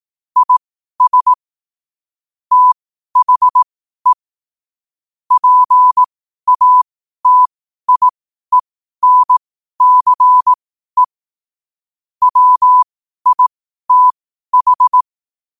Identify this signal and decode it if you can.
Morse code, heard as a keyed tone, about 18 wpm, spaced as for 12 wpm, 1000 Hz. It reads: IS THE PATIENCE WITH
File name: 12wpm